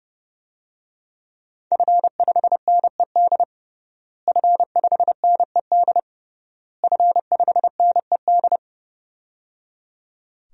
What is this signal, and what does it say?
Morse code, heard as a keyed tone, about 30 wpm, 700 Hz. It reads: F5NED F5NED F5NED
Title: DXCC-100 Call Signs - 5 Characters - Repeated 3x - 30wpm